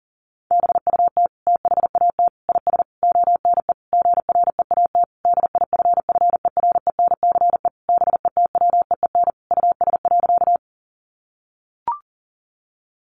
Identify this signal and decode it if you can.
Morse code, heard as a keyed tone, about 40 wpm, 700 Hz. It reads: BUT THAT IS ONE GREAT DIFFERENCE BETWEEN US.
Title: Sentences from Top 800 Words 40wpm